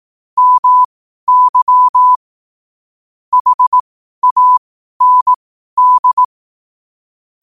Morse code, heard as a keyed tone, about 18 wpm, spaced as for 12 wpm, 1000 Hz. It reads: MY HAND
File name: qotdfiles / 12wpm